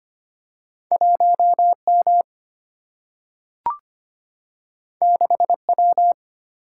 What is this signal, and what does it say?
Morse code, heard as a keyed tone, about 25 wpm, 700 Hz. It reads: 1M 6W
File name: Two Letter-Number Combinations - 25wpm